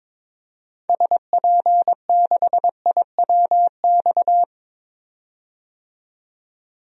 Morse code, heard as a keyed tone, about 22 wpm, 700 Hz. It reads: SP6IWX
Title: DXCC-100 Call Signs - 6 Characters - 22wpm